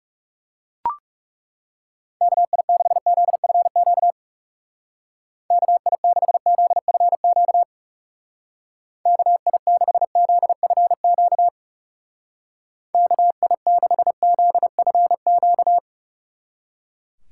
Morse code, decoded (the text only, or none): KI6ZFQ KI6ZFQ KI6ZFQ KI6ZFQ